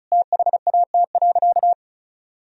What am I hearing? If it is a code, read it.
Morse code, heard as a keyed tone, about 34 wpm, 700 Hz. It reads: THAT.
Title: Sentences from Top 1000 Words - Review Part 2 - 35wpm